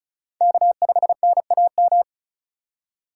Morse code, heard as a keyed tone, about 44 wpm, 700 Hz. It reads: K5NAM